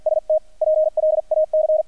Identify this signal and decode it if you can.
Morse code, heard as a keyed tone, about 48 wpm, 620 Hz, with radio noise. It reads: ST PLAC
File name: archived / CRSnet